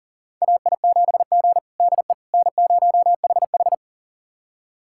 Morse code, heard as a keyed tone, about 37 wpm, 700 Hz. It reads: AI7G DE N0HH
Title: USA Callsign DE Callsign - 4 Characters - 40wpm